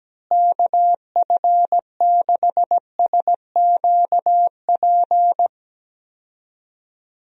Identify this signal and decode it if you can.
Morse code, heard as a keyed tone, about 17 wpm, 700 Hz. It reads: KF6SQP
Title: USA Call Signs - 6 Characters - Repeated 2x - 17wpm